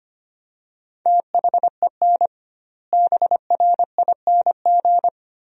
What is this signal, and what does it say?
Morse code, heard as a keyed tone, about 25 wpm, 700 Hz. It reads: THEN BRING
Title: 2x Word-Spacing - Sentences from Top 500 Words - 25wpm